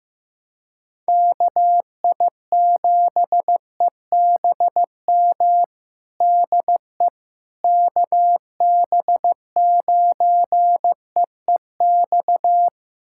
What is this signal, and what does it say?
Morse code, heard as a keyed tone, about 15 wpm, 700 Hz. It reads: KI7EBM DE KB9EEX
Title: USA Callsign DE Callsign - 6 Characters - 15wpm